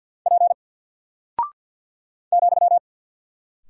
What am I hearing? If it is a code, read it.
Morse code, heard as a keyed tone, about 50 wpm, 700 Hz. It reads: P ,
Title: Single Character (Letter:Number:Punctuation:Prosign) 50wpm